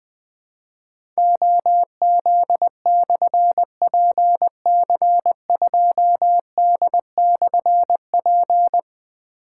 Morse code, heard as a keyed tone, about 20 wpm, 700 Hz. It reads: OZ/PC2D/P
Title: Difficult Call Signs 20wpm